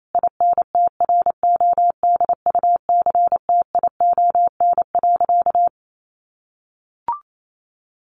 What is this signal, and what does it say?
Morse code, heard as a keyed tone, about 28 wpm, 700 Hz. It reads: INTRODUCTION.
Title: Sentences from Top 1000 Words 28wpm